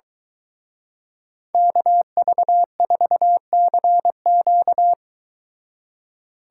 Morse code, heard as a keyed tone, about 23 wpm, 700 Hz. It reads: KV4CQ